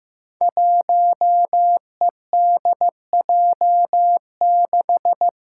Morse code, heard as a keyed tone, about 15 wpm, 700 Hz. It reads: 1EDJ6
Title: Five Challenging Letter-Number Combinations 15wpm